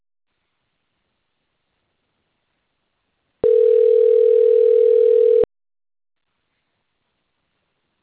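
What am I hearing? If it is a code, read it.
Ringback tone (North America), 440 + 480 Hz.
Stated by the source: Telephone call progress tone on an outgoing call recorded in Itatiba.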